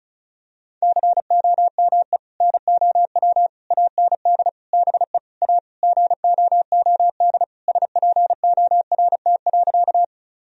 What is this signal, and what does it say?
Morse code, heard as a keyed tone, about 36 wpm, 700 Hz. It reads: COME NOW AND BE A GOOD SPORT.
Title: Sentences from Top 1000 Words - Review Part 4 - 35wpm